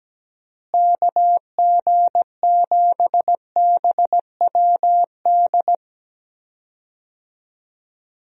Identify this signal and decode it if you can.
Morse code, heard as a keyed tone, about 17 wpm, 700 Hz. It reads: KG7BWD
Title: USA Call Signs - 6 Characters - Repeated 3x - 17wpm